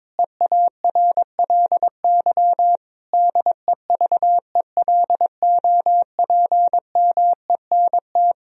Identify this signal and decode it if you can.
Morse code, heard as a keyed tone, about 22 wpm, 700 Hz. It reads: EARLY DEVELOPMENT
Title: Sentences from Top 1000 Words - Review Part 4 - 22wpm